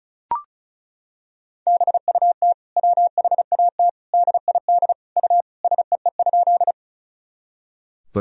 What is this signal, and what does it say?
Morse code, heard as a keyed tone, about 36 wpm, 700 Hz. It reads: BUT WHAT DID U SEE?
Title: Sentences from Top 100 Words and Abbreviations 35wpm